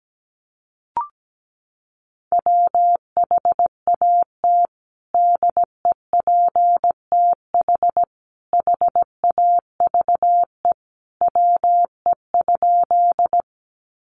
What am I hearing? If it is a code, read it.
Morse code, heard as a keyed tone, about 17 wpm, 700 Hz. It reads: WHAT DEPTH HAVE WE?
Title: Sentences from Top 1200 Words 17wpm